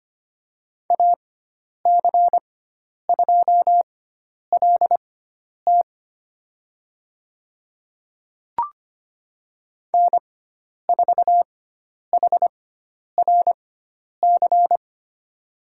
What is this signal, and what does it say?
Morse code, heard as a keyed tone, about 25 wpm, spaced as for 10 wpm, 700 Hz. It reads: AC2LT N4HRC
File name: course / 25:10wpm / 095 Lesson 16g - Letter c - Call Signs 25wpm